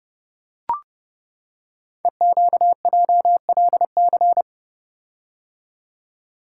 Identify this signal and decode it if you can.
Morse code, heard as a keyed tone, about 28 wpm, 700 Hz. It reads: EQJLC
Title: Code Groups - 5 Letter:Numbers - 30wpm